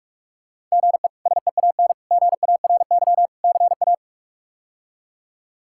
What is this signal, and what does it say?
Morse code, heard as a keyed tone, about 46 wpm, 700 Hz. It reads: GE SEAN GARY CA
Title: K1USN SST QSO Practice - 45wpm